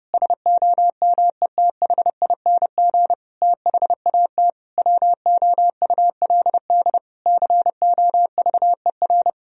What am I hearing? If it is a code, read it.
Morse code, heard as a keyed tone, about 30 wpm, 700 Hz. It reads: SOMETHING THAT WOULD COVER
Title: Sentences from Top 1000 Words - Review Part 1 - 30wpm